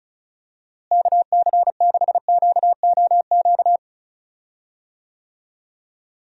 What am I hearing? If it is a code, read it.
Morse code, heard as a keyed tone, about 36 wpm, 700 Hz. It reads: KC6QOQ
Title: USA Call Signs - 6 Characters - Repeated 3x - 35wpm